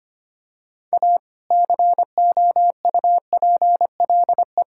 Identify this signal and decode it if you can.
Morse code, heard as a keyed tone, about 25 wpm, 700 Hz. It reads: A COUPLE
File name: practice / 25wpm / Sentences with 2-Word Phrases - 25wpm